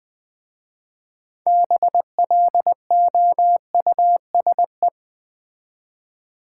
Morse code, heard as a keyed tone, about 21 wpm, 700 Hz. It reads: BLOUSE
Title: Difficult Words with Hints - 20wpm